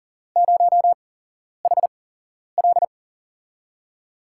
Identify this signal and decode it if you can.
Morse code, heard as a keyed tone, about 40 wpm, spaced as for 11 wpm, 700 Hz. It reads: 0HL
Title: USA Call Signs - 8x Character Spacing - 40wpm